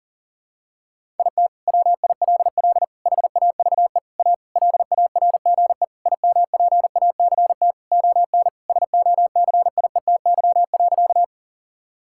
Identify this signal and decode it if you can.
Morse code, heard as a keyed tone, about 40 wpm, 700 Hz. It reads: IT WILL HAVE A LARGE IMPACT ON SOCIETY.
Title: Sentences from Top 1000 Words - Review Part 6 - 40wpm